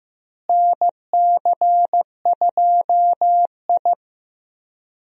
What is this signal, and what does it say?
Morse code, heard as a keyed tone, about 15 wpm, 700 Hz. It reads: NC2I
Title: USA Call Signs - 4 Characters - Repeated 2x - 15wpm